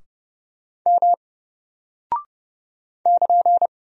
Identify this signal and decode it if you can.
Morse code, heard as a keyed tone, about 29 wpm, 700 Hz. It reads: M <KN>
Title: Single Character (Letter:Number:Punctuation:Prosign) 30wpm